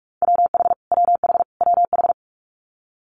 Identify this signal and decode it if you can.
Morse code, heard as a keyed tone, about 46 wpm, 690 Hz. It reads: WH WH WH
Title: Multi-letter Phonemes Repeated 3x 45wpm